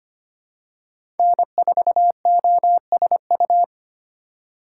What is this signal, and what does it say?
Morse code, heard as a keyed tone, about 25 wpm, 700 Hz. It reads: N4OSU
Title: USA Call Signs - 5 Characters - 2x Character Spacing - 25wpm